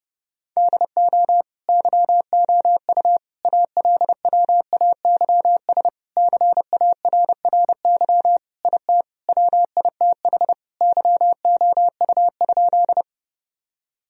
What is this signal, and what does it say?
Morse code, heard as a keyed tone, about 30 wpm, 700 Hz. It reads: DO YOU ALWAYS CARRY IT WITH YOU?
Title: Sentences from Top 1000 Words - Review Part 1 - 30wpm